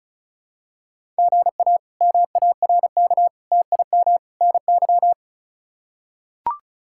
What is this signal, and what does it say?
Morse code, heard as a keyed tone, about 36 wpm, 700 Hz. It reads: GA MARK TIM NY
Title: K1USN SST QSO Practice - 35wpm